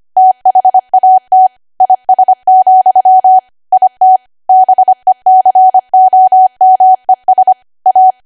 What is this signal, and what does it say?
Morse code, heard as a keyed tone, about 25 wpm, 750 Hz. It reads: THAT IS, IT BECOMES A
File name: cwa / W1AW-Level3 / 25WPM